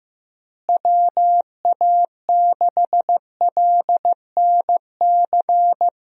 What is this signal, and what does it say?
Morse code, heard as a keyed tone, about 15 wpm, 700 Hz. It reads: WA6LNC
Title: USA Call Signs - 6 Characters - Repeated 3x - 15wpm